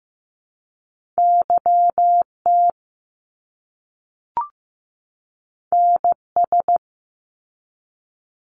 Morse code, heard as a keyed tone, about 15 wpm, 700 Hz. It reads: YT NS
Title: Canadian Provinces and Territories 15wpm